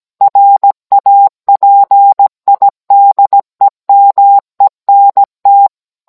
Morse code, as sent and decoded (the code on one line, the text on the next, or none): .-. .- .--. .. -.. -- . -. -
RAPIDEMENT